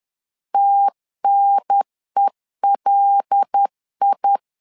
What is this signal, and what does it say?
Morse code, heard as a keyed tone, about 10 wpm, 800 Hz. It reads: TNELI
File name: LEZIONI / 5 LBVC / 5 LBV 10wpm 2248